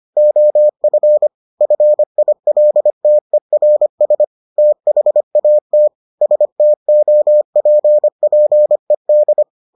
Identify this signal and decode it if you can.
Morse code, heard as a keyed tone, about 25 wpm, 600 Hz. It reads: OF FILTERS THAT STOPPED